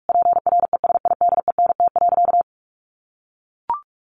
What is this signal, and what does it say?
Morse code, heard as a keyed tone, about 44 wpm, 700 Hz. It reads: PRESIDENT.
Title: Sentences from Top 1500 Words 45wpm